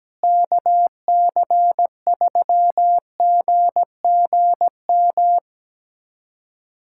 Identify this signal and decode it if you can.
Morse code, heard as a keyed tone, about 17 wpm, 700 Hz. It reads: KC3GGM
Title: USA Callsign DE Callsign - 6 Characters - 17wpm